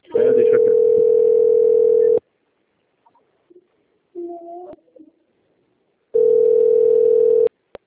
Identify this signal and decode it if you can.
Ringback tone (North America), 440 + 480 Hz.
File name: out- - - - - .33263